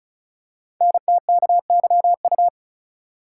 Morse code, heard as a keyed tone, about 36 wpm, 690 Hz. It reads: NTKYU
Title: Code Groups - 5 Letters - 35wpm